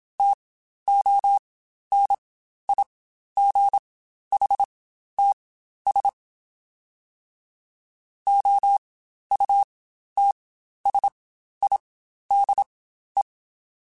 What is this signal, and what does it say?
Morse code, heard as a keyed tone, about 27 wpm, spaced as for 12 wpm, 800 Hz. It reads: TONIGHTS OUTSIDE
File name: Spaced Characters / Spaced Characters-mp3s / Weather5 5 sp @25